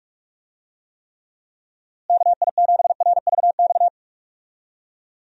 Morse code, heard as a keyed tone, about 45 wpm, 700 Hz. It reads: KI7RVX
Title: USA Call Signs - 6 Characters - Repeated 2x - 45wpm